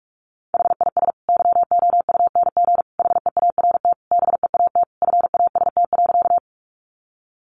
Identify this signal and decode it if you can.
Morse code, heard as a keyed tone, about 44 wpm, 700 Hz. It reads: HIS YOUNG HEART BEAT FAST.
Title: Sentences from Top 1300 Words 45wpm